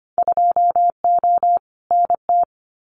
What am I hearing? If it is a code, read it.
Morse code, heard as a keyed tone, about 25 wpm, 700 Hz. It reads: 2O NT
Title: Winter Field Day Exchanges Encore 25wpm